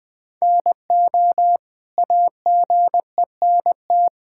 Morse code, heard as a keyed tone, about 20 wpm, 700 Hz. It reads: NO AGENT
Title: Sentences from Top 1600 Words 20wpm